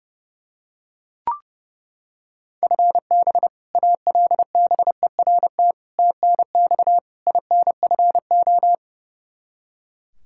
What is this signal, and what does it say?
Morse code, heard as a keyed tone, about 30 wpm, 700 Hz. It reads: FB ALBERT TNX INFO
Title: Practice QSO - Exchange 3 - More Info - 30wpm